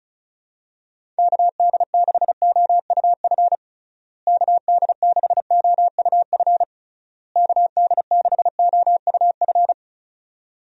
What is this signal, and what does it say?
Morse code, heard as a keyed tone, about 36 wpm, 700 Hz. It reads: KD6OUF KD6OUF KD6OUF
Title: USA Call Signs - 6 Characters - Repeated 3x - 35wpm